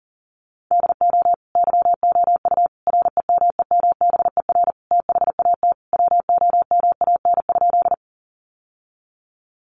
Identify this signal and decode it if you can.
Morse code, heard as a keyed tone, about 37 wpm, 700 Hz. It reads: DO YOU REMEMBER THAT WOMAN?
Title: Sentences from Top 1100 Words 40wpm